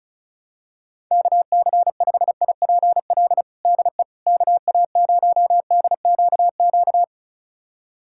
Morse code, heard as a keyed tone, about 36 wpm, 700 Hz. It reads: KC5IPL DE KA0DQQ
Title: USA Callsign DE Callsign - 6 Characters - 35wpm